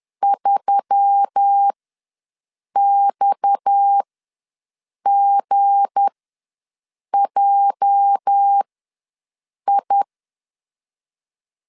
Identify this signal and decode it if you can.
Morse code, heard as a keyed tone, about 10 wpm, spaced as for 6 wpm, 800 Hz. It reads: 3XGJI
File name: LEZIONI / 8 YJ45 / 8 YJ45 ws 10wpm 3843